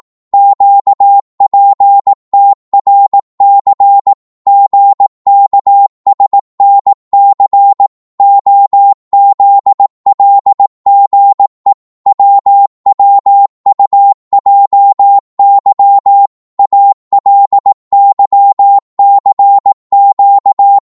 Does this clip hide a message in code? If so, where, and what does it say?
Yes: Morse code from 0.3 to 20.8 s, heard as a keyed tone, about 18 wpm, 800 Hz. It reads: QPTRC GKSNC OZLGE WWUJY ALYCQ